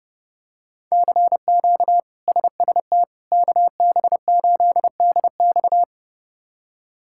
Morse code, heard as a keyed tone, about 32 wpm, 700 Hz. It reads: CQ SST KB8DX
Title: K1USN SST QSO Practice - 30wpm